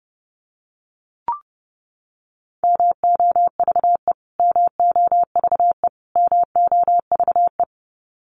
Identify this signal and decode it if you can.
Morse code, heard as a keyed tone, about 30 wpm, 700 Hz. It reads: MOVE MOVE MOVE
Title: Top 200 Words Repeated 30wpm